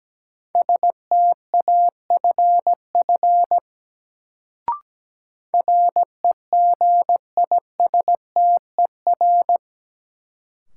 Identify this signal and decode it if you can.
Morse code, heard as a keyed tone, about 17 wpm, 700 Hz. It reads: STAFF REGISTER
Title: Sentences from Top 1200 Words - New Word Drill - 17wpm